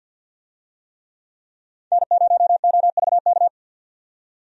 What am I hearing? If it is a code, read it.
Morse code, heard as a keyed tone, about 50 wpm, 700 Hz. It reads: N0OVK
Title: USA Call Signs - 5 Characters - 50wpm